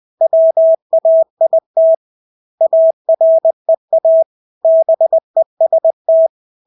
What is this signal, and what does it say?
Morse code, heard as a keyed tone, about 20 wpm, 650 Hz. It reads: WAIT AREA BEST
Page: Engelske ord 20 wpm